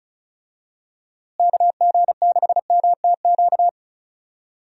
Morse code, heard as a keyed tone, about 34 wpm, 700 Hz. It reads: KG6MTQ
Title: USA Call Signs - 6 Characters - Repeated 3x - 35wpm